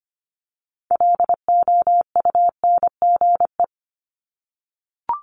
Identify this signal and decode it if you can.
Morse code, heard as a keyed tone, about 25 wpm, 700 Hz. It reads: LOUNGE
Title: Multi-letter Phonemes 25wpm